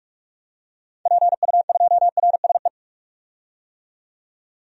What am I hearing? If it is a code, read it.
Morse code, heard as a keyed tone, about 46 wpm, 700 Hz. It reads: PU2RSE
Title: DXCC-100 Call Signs - 6 Characters - 45wpm